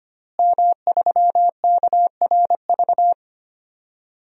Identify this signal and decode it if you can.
Morse code, heard as a keyed tone, about 25 wpm, 700 Hz. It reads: M3KRV
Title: DXCC-100 Call Signs - 5 Characters - Repeated 2x - 25wpm